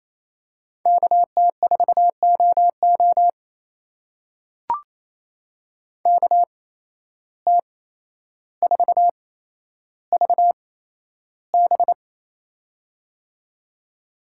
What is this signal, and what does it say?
Morse code, heard as a keyed tone, about 28 wpm, spaced as for 12 wpm, 700 Hz. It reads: KT4OO KT4VB
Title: USA Call Signs - 5 Characters - 8x Character Spacing - 28wpm